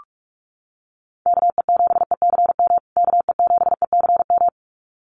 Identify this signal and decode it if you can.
Morse code, heard as a keyed tone, about 44 wpm, 700 Hz. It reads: KE7ECM KE7ECM
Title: USA Call Signs Repeated 2x 45wpm